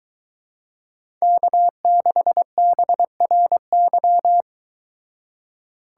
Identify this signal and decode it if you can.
Morse code, heard as a keyed tone, about 23 wpm, 700 Hz. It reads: K6BRY